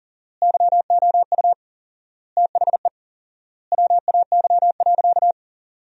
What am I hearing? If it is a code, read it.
Morse code, heard as a keyed tone, about 40 wpm, 700 Hz. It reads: YOU THE WAY.
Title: 4x Word-Spacing - Sentences from Top 300 Words - 40wpm